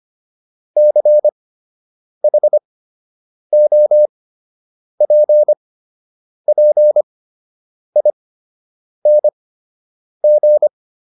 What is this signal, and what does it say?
Morse code, heard as a keyed tone, about 25 wpm, spaced as for 8 wpm, 600 Hz. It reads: CHOPPING